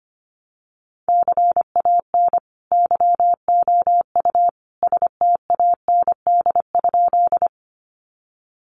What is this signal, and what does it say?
Morse code, heard as a keyed tone, about 25 wpm, 700 Hz. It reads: CAN YOU STAND?
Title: Sentences from Top 300 Words 25wpm